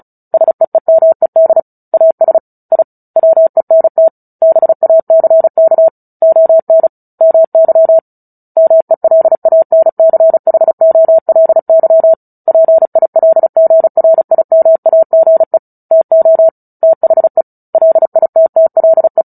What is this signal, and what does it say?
Morse code, heard as a keyed tone, about 34 wpm, 650 Hz. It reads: SEEMED AS I WENT BACK ON MY MELANCHOLY PILGRIMAGE TO THE LITTLE
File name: Book Two / 09-Chapter 9 - 35 WPM